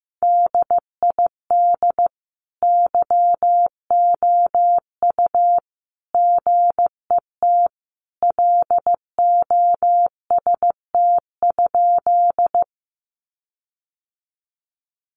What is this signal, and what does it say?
Morse code, heard as a keyed tone, about 15 wpm, 700 Hz. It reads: DID YOU GET LOST?
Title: Sentences from Top 1400 Words 15wpm